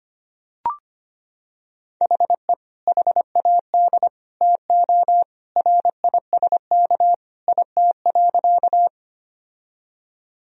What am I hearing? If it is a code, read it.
Morse code, heard as a keyed tone, about 25 wpm, 700 Hz. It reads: HE HAD TO RISK IT.
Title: Sentences from Top 1000 Words - Review Part 4 - 25wpm